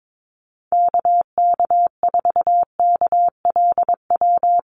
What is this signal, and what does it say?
Morse code, heard as a keyed tone, about 22 wpm, 700 Hz. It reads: KK4KLW
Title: USA Call Signs Repeated 3x 22wpm